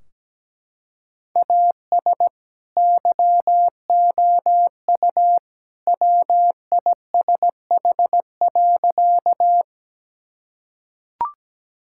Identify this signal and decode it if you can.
Morse code, heard as a keyed tone, about 17 wpm, 700 Hz. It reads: AS YOU WISH.
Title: Sentences from Top 1600 Words 17wpm